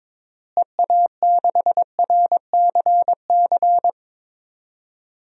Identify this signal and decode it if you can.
Morse code, heard as a keyed tone, about 22 wpm, 700 Hz. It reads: EA6RCC
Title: DXCC-100 Call Signs - 6 Characters - Repeated 3x - 22wpm